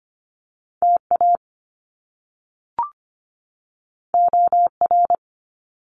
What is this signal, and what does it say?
Morse code, heard as a keyed tone, about 25 wpm, 700 Hz. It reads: TA OR
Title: Top Two Letter Combinations in Words 25wpm